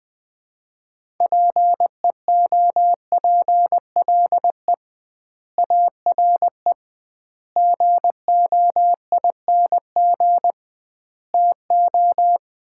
Morse code, heard as a keyed tone, about 20 wpm, 700 Hz. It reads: PEOPLE ARE GOING TO
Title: 2x Word-Spacing - Sets of 4 Words - 20wpm